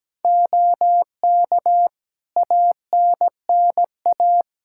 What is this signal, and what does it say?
Morse code, heard as a keyed tone, about 17 wpm, 700 Hz. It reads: OK ANNA
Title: Practice QSO - Exchange 2 - Info - 17wpm